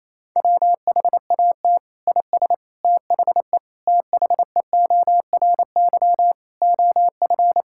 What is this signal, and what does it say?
Morse code, heard as a keyed tone, about 28 wpm, 700 Hz. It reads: WHAT IS THE THEORY OF
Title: Sentences from Top 1000 Words - Review Part 6 - 28wpm